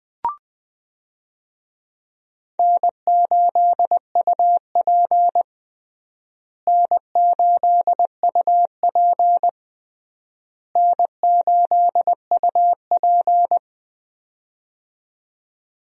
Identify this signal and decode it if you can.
Morse code, heard as a keyed tone, about 21 wpm, 700 Hz. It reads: N8UP N8UP N8UP
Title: USA Call Signs - 4 Characters - Repeated 3x - 20wpm